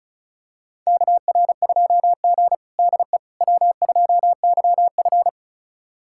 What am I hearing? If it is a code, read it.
Morse code, heard as a keyed tone, about 36 wpm, 700 Hz. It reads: KR2G DE W2YF
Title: USA Callsign DE Callsign - 4 Characters - 35wpm